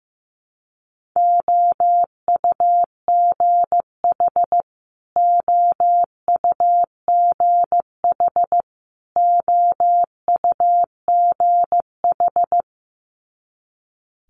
Morse code, heard as a keyed tone, about 15 wpm, 700 Hz. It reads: OUGH OUGH OUGH
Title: Multi-letter Phonemes Repeated 3x 15wpm